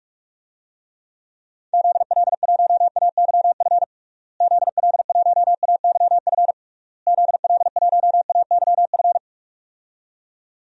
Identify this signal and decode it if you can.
Morse code, heard as a keyed tone, about 46 wpm, 700 Hz. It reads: ZL1AYF ZL1AYF ZL1AYF
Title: DXCC-100 Call Signs - 6 Characters - Repeated 3x - 45wpm